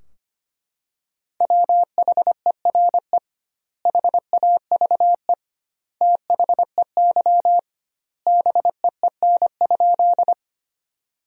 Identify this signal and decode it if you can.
Morse code, heard as a keyed tone, about 25 wpm, 700 Hz. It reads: WHERE HAVE THEY BEEN?
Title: 2x Word-Spacing - Sentences from Top 200 Words - 25wpm